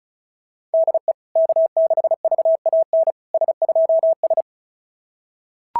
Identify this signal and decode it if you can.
Morse code, heard as a keyed tone, about 36 wpm, 660 Hz. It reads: DE K6VAN S2S
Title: SOTA QSO Practice 35wpm